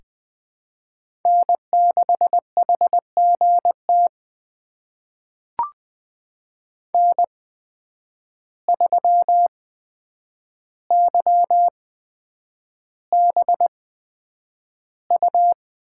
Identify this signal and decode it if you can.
Morse code, heard as a keyed tone, about 20 wpm, spaced as for 9 wpm, 700 Hz. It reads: N6HGT N3YBU
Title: USA Call Signs - 5 Characters - 8x Character Spacing - 20wpm